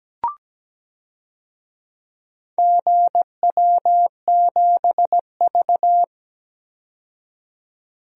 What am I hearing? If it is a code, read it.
Morse code, heard as a keyed tone, about 17 wpm, 700 Hz. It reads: GW7V
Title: DXCC-100 Call Signs - 4 Characters - 17wpm